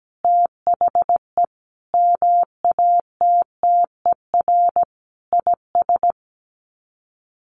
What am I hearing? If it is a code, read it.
Morse code, heard as a keyed tone, about 17 wpm, 700 Hz. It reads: THE MATTER IS
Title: Sets of 5 Words 17wpm